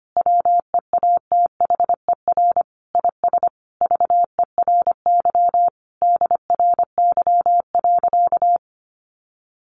Morse code, heard as a keyed tone, about 25 wpm, 700 Hz. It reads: WEATHER IS VERY DRY.
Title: Sentences from Top 1400 Words 25wpm